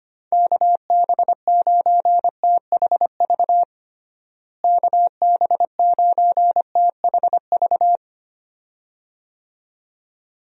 Morse code, heard as a keyed tone, about 25 wpm, 700 Hz. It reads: KB9THV KB9THV
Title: USA Call Signs - 6 Characters - Repeated 2x - 25wpm